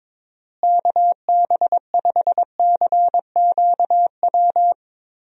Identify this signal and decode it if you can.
Morse code, heard as a keyed tone, about 22 wpm, 700 Hz. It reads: KB5CQW
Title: USA Call Signs - 6 Characters - Repeated 2x - 22wpm